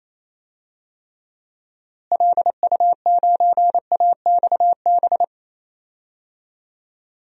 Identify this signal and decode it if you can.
Morse code, heard as a keyed tone, about 27 wpm, 700 Hz. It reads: LU9AXB
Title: DXCC-100 Call Signs - 6 Characters - Repeated 2x - 28wpm